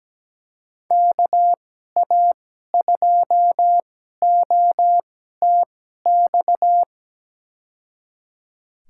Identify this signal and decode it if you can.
Morse code, heard as a keyed tone, about 17 wpm, spaced as for 12 wpm, 700 Hz. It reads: KA2OTX
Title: USA Call Signs - 2x Character Spacing - 17wpm